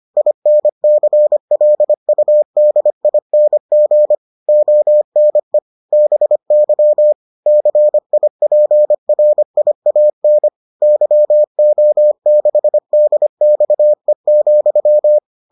Morse code, heard as a keyed tone, about 25 wpm, 600 Hz. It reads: INCLUDING ONE BY CIPRIAN YO6DXE,